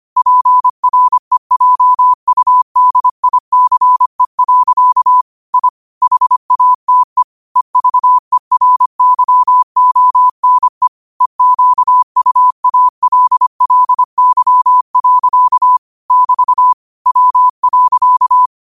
Morse code, heard as a keyed tone, about 25 wpm, 1000 Hz. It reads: PREJUDICE. I HATE EVERYONE EQUALLY. <BT> W.
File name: qotdfiles / 25wpm